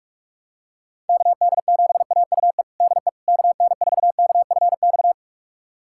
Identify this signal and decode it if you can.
Morse code, heard as a keyed tone, about 46 wpm, 700 Hz. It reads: KD7AUE DE KN4KFX
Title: USA Callsign DE Callsign - 6 Characters - 45wpm